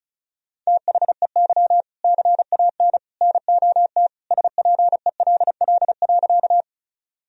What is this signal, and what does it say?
Morse code, heard as a keyed tone, about 36 wpm, 700 Hz. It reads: THEY CAN NOT SPELL.
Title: Sentences from Top 1000 Words - Review Part 1 - 35wpm